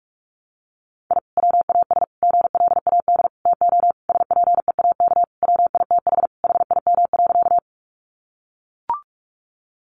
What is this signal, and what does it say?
Morse code, heard as a keyed tone, about 44 wpm, 700 Hz. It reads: I WAS GLAD TO SPEAK WITH HIM.
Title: Sentences from Top 1100 Words 45wpm